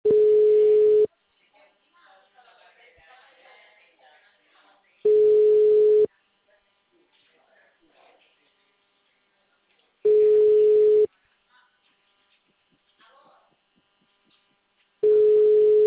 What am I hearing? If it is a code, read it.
Ringback tone (Europe, 425 Hz), with speech.
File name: out- -2056- - - .7359